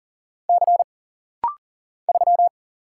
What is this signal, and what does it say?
Morse code, heard as a keyed tone, about 40 wpm, 700 Hz. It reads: C 3
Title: Single Letter-Number Alt - Rapid-Fire 40wpm